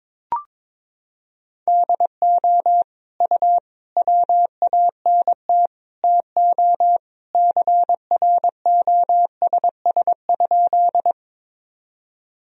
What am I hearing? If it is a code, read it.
Morse code, heard as a keyed tone, about 22 wpm, 700 Hz. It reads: DO U WANT TO CROSS?
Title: Sentences from Top 300 Words and Abbreviations 22wpm